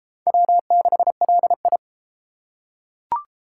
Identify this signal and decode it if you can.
Morse code, heard as a keyed tone, about 31 wpm, 700 Hz. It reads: W6LI